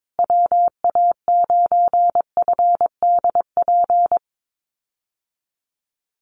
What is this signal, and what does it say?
Morse code, heard as a keyed tone, about 22 wpm, 700 Hz. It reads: WA9FDP
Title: USA Call Signs Repeated 2x 22wpm